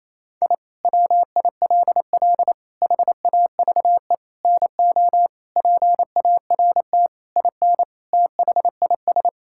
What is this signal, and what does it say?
Morse code, heard as a keyed tone, about 28 wpm, 700 Hz. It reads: I WILL HAVE NO PART IN THIS
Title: Sentences from Top 1000 Words - Review Part 5 - 28wpm